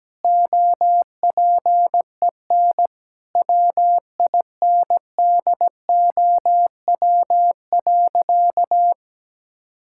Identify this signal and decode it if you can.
Morse code, heard as a keyed tone, about 17 wpm, 700 Hz. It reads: OPEN WINDOW.
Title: Sentences from Top 1600 Words 17wpm